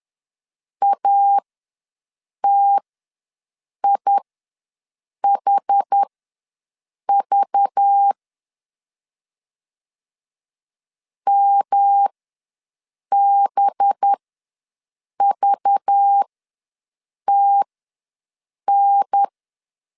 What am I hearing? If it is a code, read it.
Morse code, heard as a keyed tone, about 10 wpm, spaced as for 6 wpm, 800 Hz. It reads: ATIHV MBVTN
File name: LEZIONI / 5 LBVC / 5 LBV ws 10wpm 3645